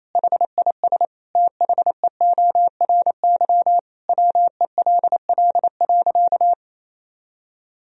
Morse code, heard as a keyed tone, about 28 wpm, 700 Hz. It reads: HIS THEORY WELL.
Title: Sentences from Top 1000 Words - Review Part 3 - 28wpm